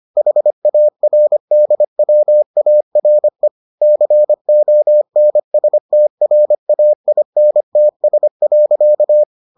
Morse code, heard as a keyed tone, about 25 wpm, 600 Hz. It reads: HARDWARE CONSTRAINTS.